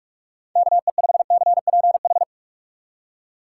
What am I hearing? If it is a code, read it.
Morse code, heard as a keyed tone, about 46 wpm, 700 Hz. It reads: KE5CPH